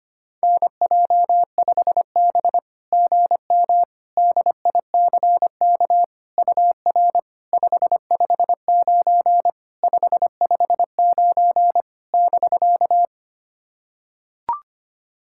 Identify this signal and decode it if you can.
Morse code, heard as a keyed tone, about 25 wpm, 690 Hz. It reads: NJ5B GM DICK UR 559 559 <BK>
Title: SOTA QSO Practice 25wpm